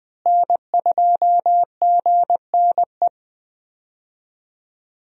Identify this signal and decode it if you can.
Morse code, heard as a keyed tone, about 20 wpm, 700 Hz. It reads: N2GNE
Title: USA Call Signs - 5 Characters - Repeated 3x - 20wpm